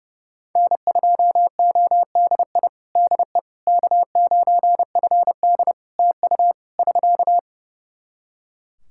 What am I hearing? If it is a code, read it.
Morse code, heard as a keyed tone, about 32 wpm, 700 Hz. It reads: N2ODI DE K9FD TU <SK>
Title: Practice QSO - Exchange 4 - Ending - 30wpm